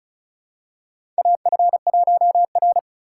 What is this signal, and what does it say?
Morse code, heard as a keyed tone, about 34 wpm, 700 Hz. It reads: AF1R
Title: USA Call Signs - 4 Characters - 3x Character Spacing - 35wpm